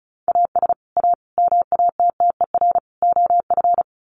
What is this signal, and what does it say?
Morse code, heard as a keyed tone, about 34 wpm, 700 Hz. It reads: AS A MATTER OF
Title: Sets of 4 Words 35wpm